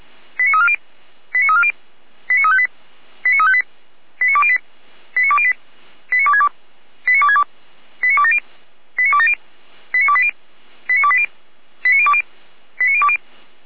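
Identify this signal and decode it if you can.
Five-tone selective call (ZVEI), heard as quick runs of five tones, tones of about 70 ms. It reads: 89340 89340 89358 89358 89208 89208 89262 89262 89270 89270 89280 89280 89020 89020
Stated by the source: ZVEI five-tone selective call received by SDR.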